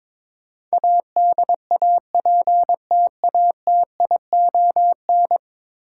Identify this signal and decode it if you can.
Morse code, heard as a keyed tone, about 21 wpm, 700 Hz. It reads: ADAPTATION
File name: practice / 22wpm / 10 Letter Words - First 3 Letters Given - 22wpm